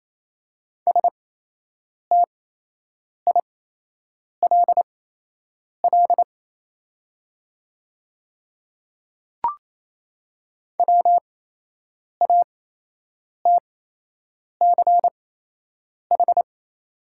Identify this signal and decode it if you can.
Morse code, heard as a keyed tone, about 28 wpm, spaced as for 8 wpm, 700 Hz. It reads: STILL WATCH
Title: 8x Character Spacing - Top 300 Words - 28wpm